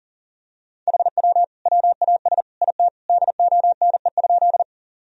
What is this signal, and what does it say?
Morse code, heard as a keyed tone, about 40 wpm, 700 Hz. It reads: HW WAS IT DONE?
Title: Sentences from Top 500 Words and Abbreviations 40wpm